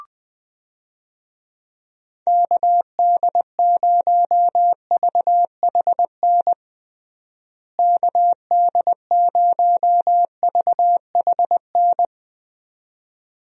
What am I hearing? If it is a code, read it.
Morse code, heard as a keyed tone, about 21 wpm, 700 Hz. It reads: KD0VHN KD0VHN
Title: USA Call Signs - 6 Characters - Repeated 2x - 20wpm